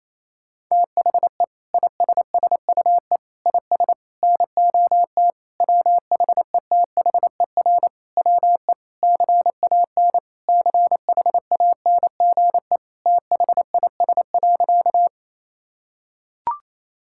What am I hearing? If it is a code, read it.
Morse code, heard as a keyed tone, about 28 wpm, 700 Hz. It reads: THE ISSUE IS NOT WHETHER WE CAN CHANGE THIS.
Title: Sentences from Top 1000 Words - Review Part 6 - 28wpm